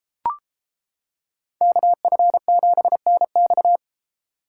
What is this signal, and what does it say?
Morse code, heard as a keyed tone, about 31 wpm, 700 Hz. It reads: KF7NX